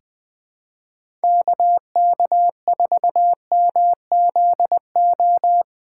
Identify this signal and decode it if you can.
Morse code, heard as a keyed tone, about 20 wpm, 700 Hz. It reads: KK4MZO
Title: USA Call Signs - 6 Characters - 4x Character Spacing - 20wpm